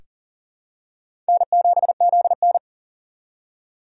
Morse code, heard as a keyed tone, about 40 wpm, 700 Hz. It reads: N7ZN
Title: K1USN SST QSO Practice - 40wpm